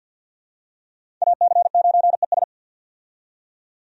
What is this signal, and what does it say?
Morse code, heard as a keyed tone, about 46 wpm, 700 Hz. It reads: AC9ES
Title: USA Call Signs - 5 Characters - 50wpm